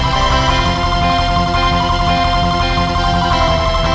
There is no code.